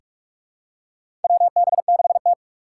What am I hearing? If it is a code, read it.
Morse code, heard as a keyed tone, about 46 wpm, 700 Hz. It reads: WB6T
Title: USA Call Signs - 4 Characters - 3x Character Spacing - 45wpm